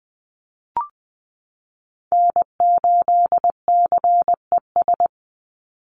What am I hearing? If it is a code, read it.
Morse code, heard as a keyed tone, about 19 wpm, 700 Hz. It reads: N8CES
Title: USA Call Signs 20wpm